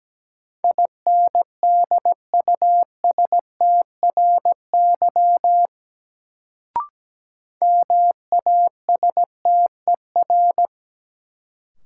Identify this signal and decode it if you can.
Morse code, heard as a keyed tone, about 17 wpm, 700 Hz. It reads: INDUSTRY MASTER
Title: Sentences from Top 1300 Words - New Word Drill Alt - 17wpm